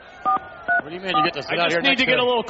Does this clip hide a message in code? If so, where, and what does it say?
Yes: DTMF from 0.3 to 1.3 s, heard as telephone touch tones, tones of about 120 ms, with speech. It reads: 434